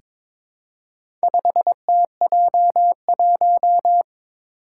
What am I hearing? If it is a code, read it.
Morse code, heard as a keyed tone, about 22 wpm, 700 Hz. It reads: 5TJ1
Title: Four Challenging Letter-Number Combinations 22wpm